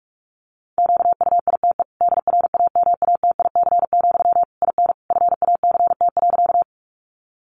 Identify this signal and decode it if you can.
Morse code, heard as a keyed tone, about 44 wpm, 700 Hz. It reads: QUITE DRAMATIC, IN FACT.
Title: Sentences from Top 1300 Words 45wpm